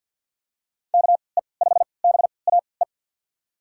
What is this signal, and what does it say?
Morse code, heard as a keyed tone, about 50 wpm, spaced as for 28 wpm, 700 Hz. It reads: KE5BAE
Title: USA Call Signs - 6 Characters - 3x Character Spacing - 50wpm